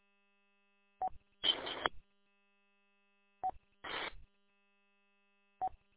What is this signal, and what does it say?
There is no code.